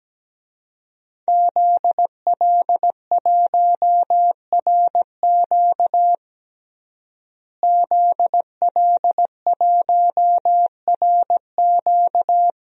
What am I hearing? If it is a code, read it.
Morse code, heard as a keyed tone, about 17 wpm, 700 Hz. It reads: ZL1RQ ZL1RQ
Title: DXCC-100 Call Signs - 5 Characters - Repeated 2x - 17wpm